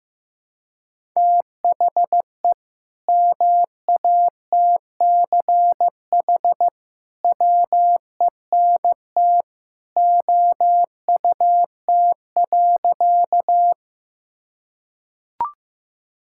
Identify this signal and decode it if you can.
Morse code, heard as a keyed tone, about 16 wpm, 700 Hz. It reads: THE MATCH WENT OUT.
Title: Sentences from Top 1600 Words 15wpm